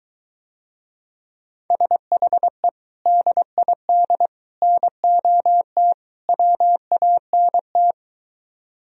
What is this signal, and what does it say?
Morse code, heard as a keyed tone, about 23 wpm, 700 Hz. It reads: SHE DID NOT WANT